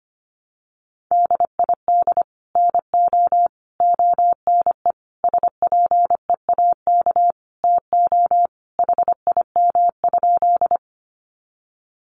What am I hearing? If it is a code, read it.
Morse code, heard as a keyed tone, about 25 wpm, 700 Hz. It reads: DID NO ONE SPEAK TO HIM?
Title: Sentences from Top 1100 Words 25wpm